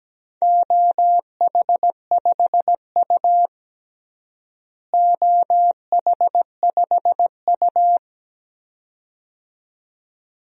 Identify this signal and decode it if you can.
Morse code, heard as a keyed tone, about 17 wpm, 700 Hz. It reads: OH5U OH5U
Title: DXCC-100 Call Signs - 4 Characters - Repeated 2x - 17wpm